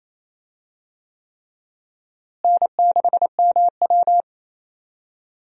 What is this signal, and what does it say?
Morse code, heard as a keyed tone, about 27 wpm, 700 Hz. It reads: N6MW
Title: USA Call Signs - 4 Characters - Repeated 3x - 28wpm